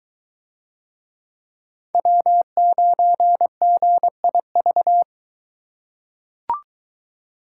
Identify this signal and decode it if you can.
Morse code, heard as a keyed tone, about 23 wpm, 700 Hz. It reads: W9GIV